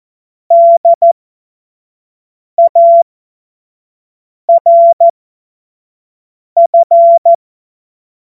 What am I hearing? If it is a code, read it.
Morse code, heard as a keyed tone, about 14 wpm, spaced as for 5 wpm, 680 Hz. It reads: DARF
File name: Morsekurs / Witze / Witz 03. Januar 2023 25 F70